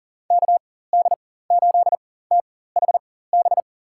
Morse code, heard as a keyed tone, about 40 wpm, spaced as for 19 wpm, 700 Hz. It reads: KD8THB
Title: USA Call Signs - 6 Characters - 4x Character Spacing - 40wpm